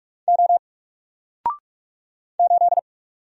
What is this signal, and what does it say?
Morse code, heard as a keyed tone, about 48 wpm, 700 Hz. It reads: O 8
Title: Single Character (Letter:Number:Punctuation:Prosign) Alt 45wpm